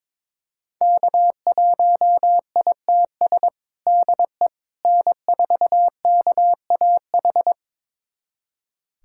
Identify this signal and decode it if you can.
Morse code, heard as a keyed tone, about 21 wpm, 700 Hz. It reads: K1ITS DE N4KAH
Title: USA Callsign DE Callsign - 5 Characters - 22wpm